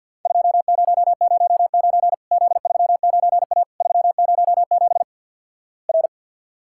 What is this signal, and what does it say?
Morse code, heard as a keyed tone, about 50 wpm, 700 Hz. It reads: 2009 Z39A 307 R
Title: Worked All Europe DX Contest - QTC Exchange - 50wpm